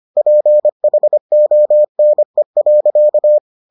Morse code, heard as a keyed tone, about 25 wpm, 600 Hz. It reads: PHONE.